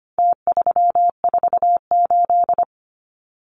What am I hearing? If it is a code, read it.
Morse code, heard as a keyed tone, about 25 wpm, 700 Hz. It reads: T348
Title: Numbers - Contest Style - 4-Digit 25wpm